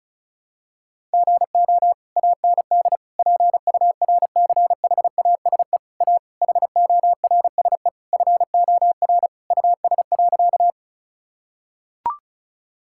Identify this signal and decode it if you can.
Morse code, heard as a keyed tone, about 36 wpm, 700 Hz. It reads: GO AND PURCHASE A HORSE FOR US.
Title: Sentences from Top 1000 Words - Review Part 5 - 35wpm